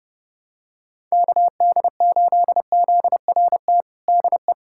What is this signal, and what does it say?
Morse code, heard as a keyed tone, about 32 wpm, 700 Hz. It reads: KD8ZRT DE
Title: USA Callsign DE Callsign 30wpm